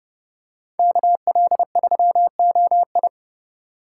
Morse code, heard as a keyed tone, about 29 wpm, 700 Hz. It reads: KL3OI